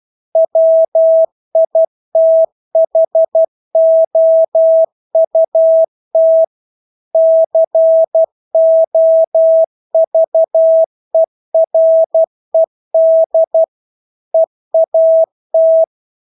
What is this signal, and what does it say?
Morse code, heard as a keyed tone, about 12 wpm, 650 Hz. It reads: WITHOUT COVERED EAT
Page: Engelske ord 12 wpm